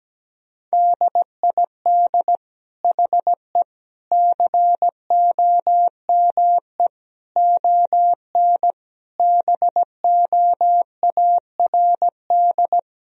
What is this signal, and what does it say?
Morse code, heard as a keyed tone, about 17 wpm, 700 Hz. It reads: DID HE COME ON BOARD
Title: Sentences from Top 1000 Words - Review Part 4 - 17wpm